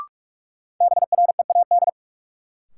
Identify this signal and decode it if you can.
Morse code, heard as a keyed tone, about 42 wpm, 700 Hz. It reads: BREAD
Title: Sentences from Top 1600 Words - New Word Drill - 45wpm